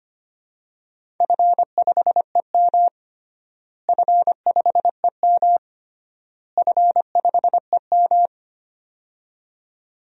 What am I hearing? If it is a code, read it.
Morse code, heard as a keyed tone, about 25 wpm, 700 Hz. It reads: F5EM F5EM F5EM
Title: DXCC-100 Call Signs - 4 Characters - Repeated 3x - 25wpm